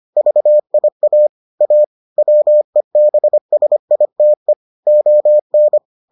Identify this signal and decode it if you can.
Morse code, heard as a keyed tone, about 25 wpm, 600 Hz. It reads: VIA A WEBSITE ON